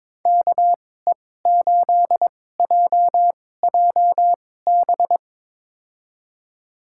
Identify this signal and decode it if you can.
Morse code, heard as a keyed tone, about 22 wpm, spaced as for 16 wpm, 700 Hz. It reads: KE8JJB
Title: USA Call Signs - 6 Characters - 2x Character Spacing - 22wpm